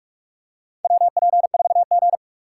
Morse code, heard as a keyed tone, about 46 wpm, 700 Hz. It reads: WP4G
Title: USA Call Signs - 4 Characters - 4x Character Spacing - 45wpm